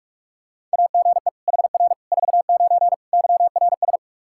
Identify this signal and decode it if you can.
Morse code, heard as a keyed tone, about 41 wpm, 700 Hz. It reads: AGE HR 49 YRS
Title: K1USN SST QSO Practice 45wpm